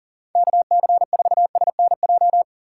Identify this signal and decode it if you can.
Morse code, heard as a keyed tone, about 40 wpm, 700 Hz. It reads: KC4SNJ
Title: USA Call Signs - 6 Characters - 6x Character Spacing - 40wpm